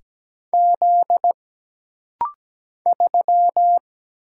Morse code, heard as a keyed tone, about 17 wpm, 700 Hz. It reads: Z 3
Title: Single Letter-Number Alt - Rapid-Fire 17wpm